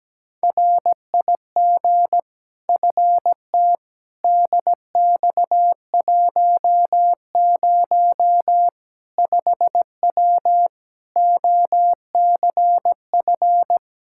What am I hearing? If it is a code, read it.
Morse code, heard as a keyed tone, about 17 wpm, 700 Hz. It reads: RIG FT DX10 5W OCF
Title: Practice QSO - Exchange 2 - Info - 17wpm